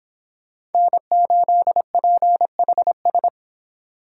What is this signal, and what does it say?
Morse code, heard as a keyed tone, about 25 wpm, 700 Hz. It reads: N8PHS